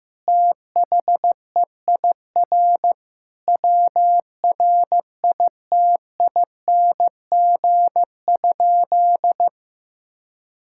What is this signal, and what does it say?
Morse code, heard as a keyed tone, about 15 wpm, 700 Hz. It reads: THEIR WRITING?
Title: Sentences from Top 1000 Words - Review Part 6 - 15wpm